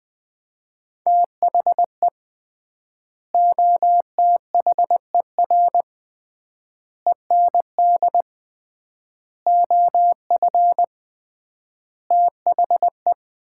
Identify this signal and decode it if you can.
Morse code, heard as a keyed tone, about 20 wpm, 700 Hz. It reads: THE OTHER END OF THE
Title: 3x Word-Spacing - Sets of 5 Words - 20wpm